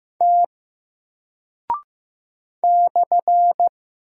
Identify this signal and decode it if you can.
Morse code, heard as a keyed tone, about 15 wpm, 700 Hz. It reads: T /
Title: Single Character (Letter:Number:Punctuation:Prosign) Alt 15wpm